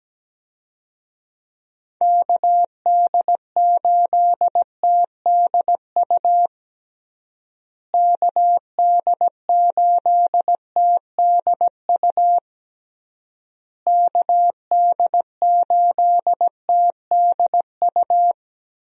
Morse code, heard as a keyed tone, about 17 wpm, 700 Hz. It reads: KD8TDU KD8TDU KD8TDU
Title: USA Call Signs - 6 Characters - Repeated 3x - 17wpm